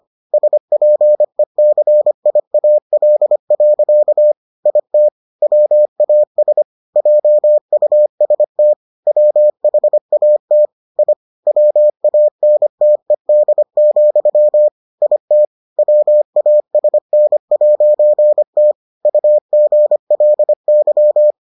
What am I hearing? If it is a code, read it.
Morse code, heard as a keyed tone, about 25 wpm, 600 Hz. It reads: SPECIAL. IT WAS JUST WHAT I WANTED, IT WASN'T UGLY